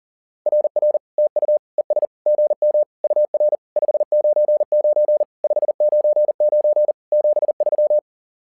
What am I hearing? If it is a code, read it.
Morse code, heard as a keyed tone, about 40 wpm, 600 Hz. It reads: RR TU ES GM UR 599 599 73
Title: SOTA QSO Practice 40wpm